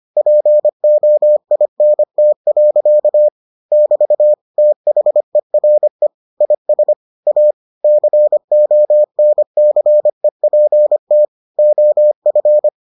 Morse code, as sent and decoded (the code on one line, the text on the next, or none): .--. --- .. -. - .-.-.- / -...- / - .... .-. / .. ... / .- / -.-. --- -. -.-. .--. - / --- ..-.
POINT. <BT> THERE IS A CONCEPT OF